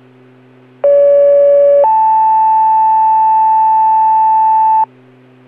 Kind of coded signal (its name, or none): two-tone page